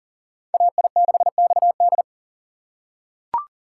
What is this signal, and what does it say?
Morse code, heard as a keyed tone, about 40 wpm, 700 Hz. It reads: AI6XD
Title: USA Call Signs - 5 Characters - 6x Character Spacing - 40wpm